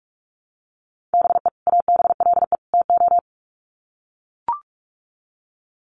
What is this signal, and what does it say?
Morse code, heard as a keyed tone, about 44 wpm, 700 Hz. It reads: BE ABLE TO
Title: Sets of 3 Words 45wpm